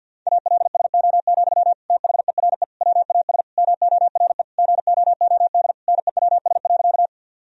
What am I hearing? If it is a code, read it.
Morse code, heard as a keyed tone, about 51 wpm, 700 Hz. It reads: ALSO, THERE WAS MORE GOOD NEWS.
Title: Sentences from Top 1000 Words - Review Part 1 - 50wpm